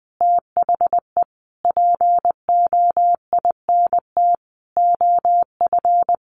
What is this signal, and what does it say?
Morse code, heard as a keyed tone, about 20 wpm, 700 Hz. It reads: THE POINT OF
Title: Sentences from Top 1300 Words 20wpm